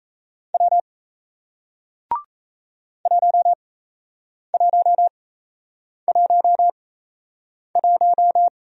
Morse code, heard as a keyed tone, about 35 wpm, spaced as for 8 wpm, 700 Hz. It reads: W 1111
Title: Speed-Racing Single Letter-Number 28wpm